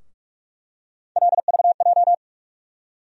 Morse code, heard as a keyed tone, about 42 wpm, 700 Hz. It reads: LVJ
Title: USA Call Sign Postfixes - 3 Letters 45wpm